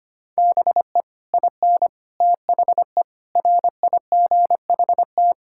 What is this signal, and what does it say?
Morse code, heard as a keyed tone, about 25 wpm, 700 Hz. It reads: BE IN THE RIGHT
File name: Sets of 9 Words - 25wpm